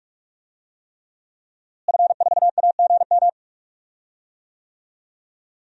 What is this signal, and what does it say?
Morse code, heard as a keyed tone, about 45 wpm, 700 Hz. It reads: F4AGM
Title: DXCC-100 Call Signs - 5 Characters - Repeated 3x - 45wpm